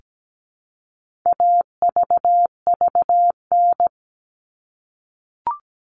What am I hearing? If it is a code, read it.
Morse code, heard as a keyed tone, about 17 wpm, 700 Hz. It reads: AVVN
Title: All Cut Numbers - 4-Digit 17wpm